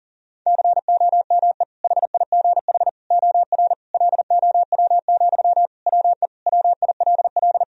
Morse code, heard as a keyed tone, about 37 wpm, 700 Hz. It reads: COME HIGH OR LOW, WE WILL
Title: Sentences from Top 1000 Words - Review Part 1 - 40wpm